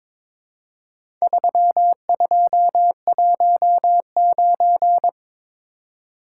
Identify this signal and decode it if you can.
Morse code, heard as a keyed tone, about 22 wpm, 700 Hz. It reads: 3219
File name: practice / 22wpm / Four Letter-Number Combinations - 22wpm